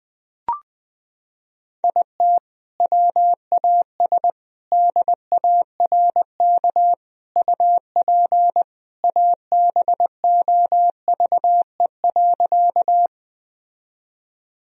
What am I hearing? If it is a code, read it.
Morse code, heard as a keyed tone, about 21 wpm, 700 Hz. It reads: IT WAS DARK UP ABOVE.
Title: Sentences from Top 1000 Words - Review Part 2 - 20wpm